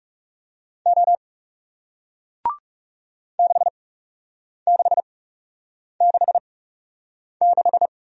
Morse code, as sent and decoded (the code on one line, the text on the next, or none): --- / -.... -.... -.... -....
O 6666